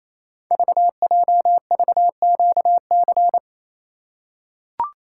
Morse code, heard as a keyed tone, about 27 wpm, 700 Hz. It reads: VJVQC
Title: Code Groups - 5 Letters - 28wpm